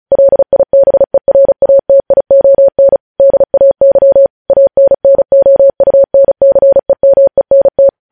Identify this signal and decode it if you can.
Morse code, heard as a keyed tone, about 36 wpm, 550 Hz. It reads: LIBERATION DAY ANNOUNCEMENT
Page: news headlines in morse code at 35 wpm